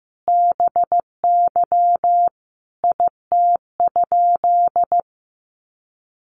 Morse code, heard as a keyed tone, about 15 wpm, 700 Hz. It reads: BY IT?
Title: Sentences from Top 100 Words 15wpm